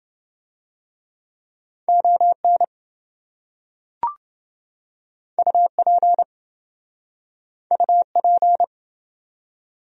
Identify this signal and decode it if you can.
Morse code, heard as a keyed tone, about 26 wpm, 700 Hz. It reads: ON UP UP